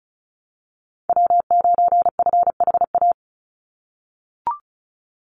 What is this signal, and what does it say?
Morse code, heard as a keyed tone, about 36 wpm, 700 Hz. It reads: W9FHA
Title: NA SOTA Chaser Callsigns 35wpm